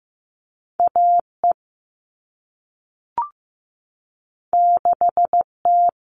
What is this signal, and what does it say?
Morse code, heard as a keyed tone, about 15 wpm, 700 Hz. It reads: AE 6T
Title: All Cut Numbers - 2-Digit 15wpm